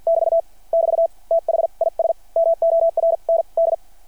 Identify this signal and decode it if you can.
Morse code, heard as a keyed tone, about 50 wpm, spaced as for 40 wpm, 660 Hz. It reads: <BT> <BT> THIS MORND
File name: archived / CRSnet